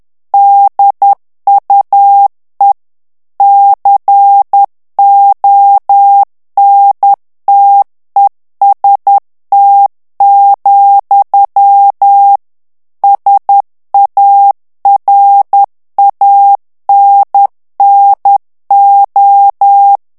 Morse code, heard as a keyed tone, about 10 wpm, 800 Hz. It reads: DUE CONTEST, SARANNO
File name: ITALIEN texte clair 2 vitesse 10 mots minute